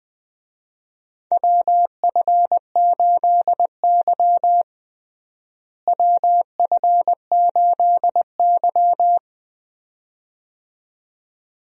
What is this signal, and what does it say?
Morse code, heard as a keyed tone, about 21 wpm, 700 Hz. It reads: WF8Y WF8Y
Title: USA Call Signs - 4 Characters - Repeated 2x - 20wpm